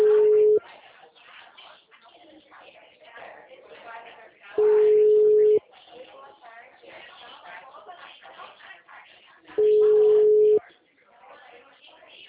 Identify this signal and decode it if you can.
Ringback tone (Europe, 425 Hz), with speech.
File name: out- -2032